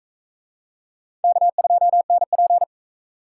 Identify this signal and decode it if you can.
Morse code, heard as a keyed tone, about 37 wpm, 700 Hz. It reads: K2NP